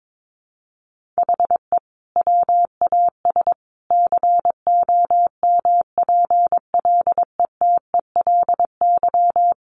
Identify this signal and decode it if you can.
Morse code, heard as a keyed tone, about 22 wpm, 700 Hz. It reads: HE WAS COMPLETELY
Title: Sentences from Top 1100 Words 22wpm